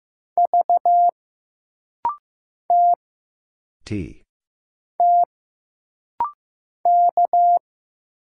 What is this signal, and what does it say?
Morse code, heard as a keyed tone, about 15 wpm, 700 Hz. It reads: V T T K
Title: Single Letter-Number Alt - Rapid-Fire 15wpm